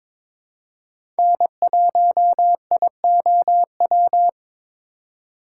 Morse code, heard as a keyed tone, about 22 wpm, 700 Hz. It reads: N1IOW
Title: USA Call Signs - 5 Characters - 4x Character Spacing - 22wpm